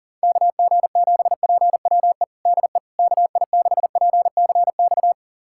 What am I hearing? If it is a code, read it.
Morse code, heard as a keyed tone, about 40 wpm, 700 Hz. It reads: KG7PWE DE KI6PCX
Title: USA Callsign DE Callsign - 6 Characters - 40wpm